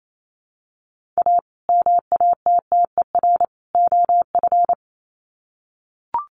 Morse code, heard as a keyed tone, about 28 wpm, 700 Hz. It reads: A MATTER OF
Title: Sets of 3 Words 28wpm